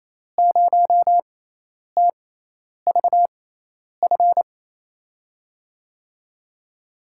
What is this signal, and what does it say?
Morse code, heard as a keyed tone, about 28 wpm, spaced as for 10 wpm, 700 Hz. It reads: 0TVF
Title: USA Call Signs - 6 Characters - 6x Character Spacing - 28wpm